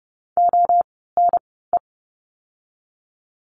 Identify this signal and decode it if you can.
Morse code, heard as a keyed tone, about 30 wpm, spaced as for 17 wpm, 700 Hz. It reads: ONE
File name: course / 30:10wpm / 108 Lesson 18f - Letter m - Words 30wpm